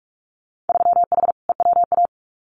Morse code, heard as a keyed tone, about 46 wpm, 700 Hz. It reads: 3H EWA
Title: Winter Field Day Exchanges Encore 45wpm